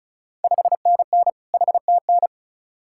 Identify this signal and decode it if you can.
Morse code, heard as a keyed tone, about 36 wpm, 700 Hz. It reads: 5NN HTN
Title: Ontario QSO Party Exchange 35wpm